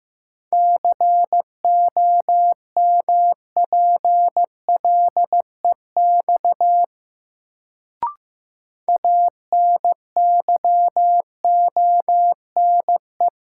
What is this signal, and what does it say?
Morse code, heard as a keyed tone, about 15 wpm, 700 Hz. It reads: COMPLEX ANYONE
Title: Sentences from Top 900 Words - New Word Drill - 15wpm